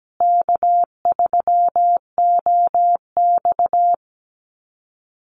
Morse code, heard as a keyed tone, about 17 wpm, 700 Hz. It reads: K3OX
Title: Call Signs Repeated 2x - 17wpm